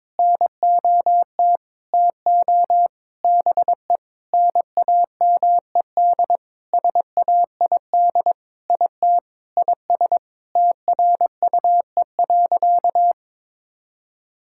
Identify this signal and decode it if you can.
Morse code, heard as a keyed tone, about 22 wpm, 700 Hz. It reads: NOT TO BE NAMED SAID IT IS TRUE.
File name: Sentences with 7-Word Phrases - 22wpm